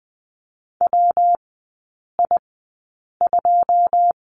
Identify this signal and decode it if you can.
Morse code, heard as a keyed tone, about 20 wpm, spaced as for 8 wpm, 690 Hz. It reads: WI2
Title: 133 Lesson 22e - Question Mark - Three Characters 20wpm